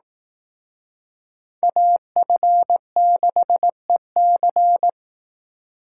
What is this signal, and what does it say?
Morse code, heard as a keyed tone, about 18 wpm, 700 Hz. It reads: AF6EC
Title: Speed Racing - USA 2x2 Call Signs 15wpm